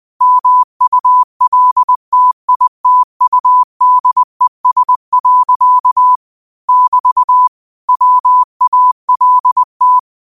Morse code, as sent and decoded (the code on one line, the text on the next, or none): -- ..- .-.. - .. - ..- -.. ... .-.-.- / -...- / .-- .- .-.. -
MULTITUDES. <BT> WALT